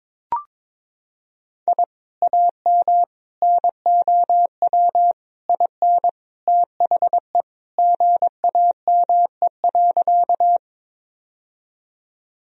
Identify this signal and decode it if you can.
Morse code, heard as a keyed tone, about 22 wpm, 700 Hz. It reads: I AM NOW IN THE GAME.
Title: Sentences from Top 1000 Words - Review Part 3 - 22wpm